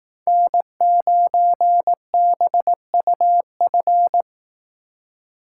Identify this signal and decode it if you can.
Morse code, heard as a keyed tone, about 18 wpm, 700 Hz. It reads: N9BUF
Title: Speed Racing - USA 1x3 Call Signs 15wpm